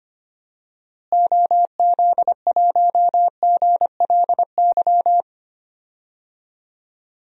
Morse code, heard as a keyed tone, about 25 wpm, 700 Hz. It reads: OZ1GLY
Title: DXCC-100 Call Signs - 6 Characters - Repeated 2x - 25wpm